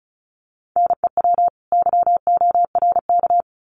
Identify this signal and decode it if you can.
Morse code, heard as a keyed tone, about 34 wpm, 700 Hz. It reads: NEW YORK
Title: Sets of 2 Words 35wpm